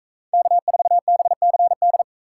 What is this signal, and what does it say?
Morse code, heard as a keyed tone, about 44 wpm, 700 Hz. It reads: K4BCD